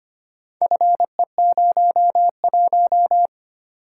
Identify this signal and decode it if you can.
Morse code, heard as a keyed tone, about 25 wpm, 700 Hz. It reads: FE01
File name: practice / 25wpm / Four Letter-Number Combinations - 25wpm